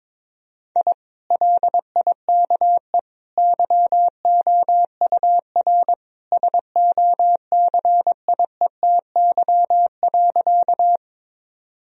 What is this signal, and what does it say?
Morse code, heard as a keyed tone, about 22 wpm, 700 Hz. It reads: I LIKE YOUR SOCIETY.
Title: Sentences from Top 1000 Words - Review Part 4 - 22wpm